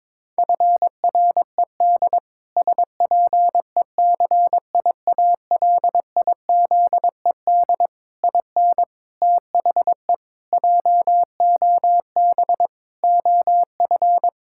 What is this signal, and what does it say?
Morse code, heard as a keyed tone, about 22 wpm, 700 Hz. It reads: FRED SPECIALIZED IN THE JOB OF
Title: Pangram Challenge v2 - 22wpm